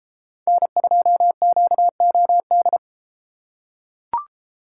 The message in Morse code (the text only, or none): N2QOD